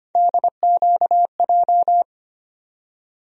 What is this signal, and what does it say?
Morse code, heard as a keyed tone, about 25 wpm, 700 Hz. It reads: DQJ
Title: USA Call Sign Postfixes - 3 Letters 25wpm